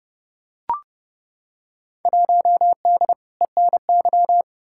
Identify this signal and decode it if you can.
Morse code, heard as a keyed tone, about 30 wpm, 700 Hz. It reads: 1D ENY
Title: Field Day Exchanges Encore - 30wpm